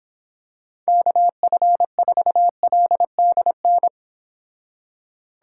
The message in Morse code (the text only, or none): KF4LDN